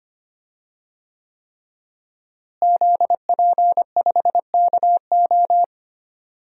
Morse code, heard as a keyed tone, about 25 wpm, 700 Hz. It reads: ZP5KO
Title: DXCC-100 Call Signs - 5 Characters - Repeated 3x - 25wpm